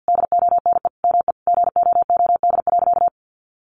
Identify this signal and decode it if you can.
Morse code, heard as a keyed tone, about 50 wpm, 700 Hz. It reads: DONE ME GOOD.
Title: Sentences from Top 500 Words 50wpm